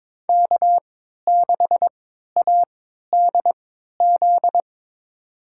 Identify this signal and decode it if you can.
Morse code, heard as a keyed tone, about 22 wpm, spaced as for 13 wpm, 700 Hz. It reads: K6ADZ
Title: USA Call Signs - 5 Characters - 3x Character Spacing - 22wpm